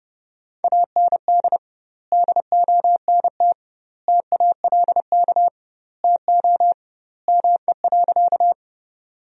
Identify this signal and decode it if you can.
Morse code, heard as a keyed tone, about 32 wpm, 700 Hz. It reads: AND DONT TALK TO ME.
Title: 2x Word-Spacing - Sentences from Top 400 Words - 30wpm